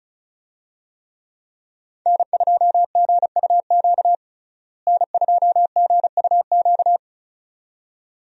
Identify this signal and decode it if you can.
Morse code, heard as a keyed tone, about 36 wpm, 700 Hz. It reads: N2GUQ N2GUQ
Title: USA Call Signs - 5 Characters - Repeated 2x - 35wpm